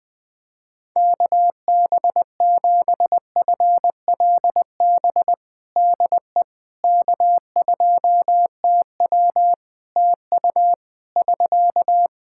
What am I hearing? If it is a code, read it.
Morse code, heard as a keyed tone, about 20 wpm, 700 Hz. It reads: KB7FLB DE K2TW TU <SK>
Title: Practice QSO - Exchange 4 - Ending - 20wpm